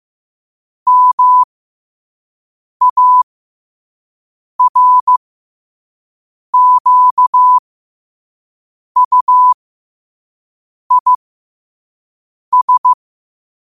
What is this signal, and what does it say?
Morse code, heard as a keyed tone, about 15 wpm, spaced as for 5 wpm, 1000 Hz. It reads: MARQUIS
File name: qotdfiles / 5wpm